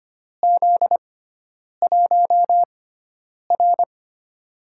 Morse code, heard as a keyed tone, about 25 wpm, spaced as for 9 wpm, 700 Hz. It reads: Z1R
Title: USA Call Signs - 6x Character Spacing - 25wpm